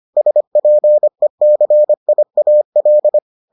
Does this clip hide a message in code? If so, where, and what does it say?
Yes: Morse code from 0.2 to 3.2 s, heard as a keyed tone, about 25 wpm, 600 Hz. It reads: SPECIAL